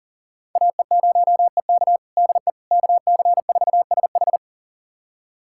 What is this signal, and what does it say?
Morse code, heard as a keyed tone, about 40 wpm, 700 Hz. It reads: AE0EK DE KC4SH
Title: USA Callsign DE Callsign - 5 Characters - 40wpm